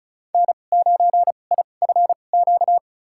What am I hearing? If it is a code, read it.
Morse code, heard as a keyed tone, about 36 wpm, spaced as for 25 wpm, 700 Hz. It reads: N9IFQ
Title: USA Call Signs - 5 Characters - 2x Character Spacing - 35wpm